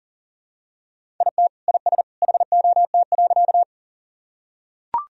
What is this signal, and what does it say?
Morse code, heard as a keyed tone, about 40 wpm, 700 Hz. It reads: IT IS HOT.
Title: Sentences from Top 1000 Words - Review Part 3 - 40wpm